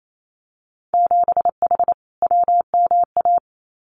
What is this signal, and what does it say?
Morse code, heard as a keyed tone, about 28 wpm, 700 Hz. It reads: 7H WMA
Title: Winter Field Day Exchanges Encore 28wpm